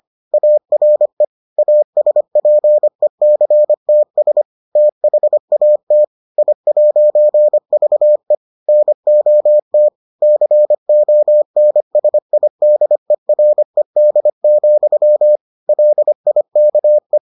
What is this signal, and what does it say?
Morse code, heard as a keyed tone, about 25 wpm, 600 Hz. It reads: ARE ASPECTS THAT I'VE NOT CONSIDERED, LIKE